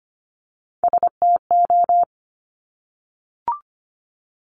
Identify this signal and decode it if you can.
Morse code, heard as a keyed tone, about 25 wpm, 700 Hz. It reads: STO
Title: Top Three Letter Combinations in Words 25wpm